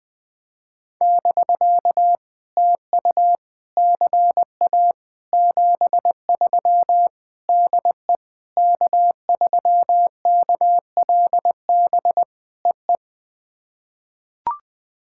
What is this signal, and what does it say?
Morse code, heard as a keyed tone, about 20 wpm, 700 Hz. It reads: <BK> TU CA 73 DE K3KLB EE
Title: POTA QSO Practice - 20wpm